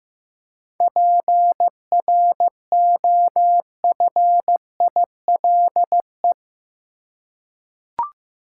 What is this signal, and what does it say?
Morse code, heard as a keyed tone, about 15 wpm, 700 Hz. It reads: PROFILE
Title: Seven Letter Words - 15wpm